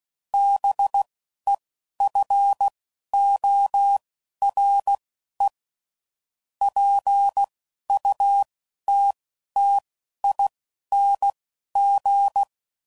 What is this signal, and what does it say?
Morse code, heard as a keyed tone, about 16 wpm, spaced as for 11 wpm, 800 Hz. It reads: BEFORE PUTTING